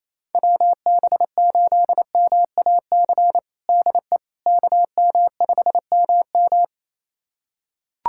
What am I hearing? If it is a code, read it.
Morse code, heard as a keyed tone, about 27 wpm, 700 Hz. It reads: WB8MAC DE KM5MM
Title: USA Callsign DE Callsign 28wpm